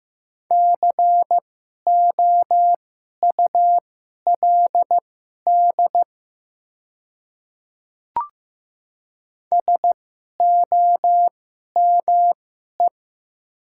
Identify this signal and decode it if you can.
Morse code, heard as a keyed tone, about 15 wpm, spaced as for 11 wpm, 700 Hz. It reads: COULD SOME
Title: 2x Character Spacing - Top 100 Words - 15wpm